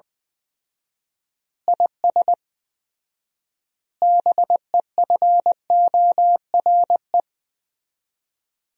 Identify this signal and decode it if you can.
Morse code, heard as a keyed tone, about 20 wpm, 700 Hz. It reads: IS BEFORE
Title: 4x Word-Spacing - Sentences from Top 200 Words - 20wpm